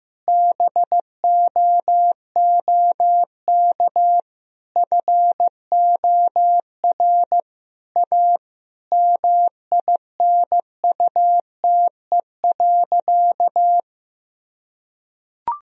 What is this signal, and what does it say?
Morse code, heard as a keyed tone, about 15 wpm, 700 Hz. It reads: BOOK FOR A MINUTE.
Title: Sentences from Top 1000 Words - Review Part 2 - 15wpm